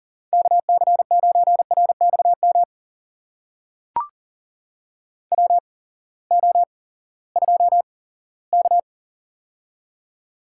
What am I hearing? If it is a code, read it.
Morse code, heard as a keyed tone, about 40 wpm, spaced as for 20 wpm, 700 Hz. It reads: KC9RXM WO2K
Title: USA Call Signs - 8x Character Spacing - 40wpm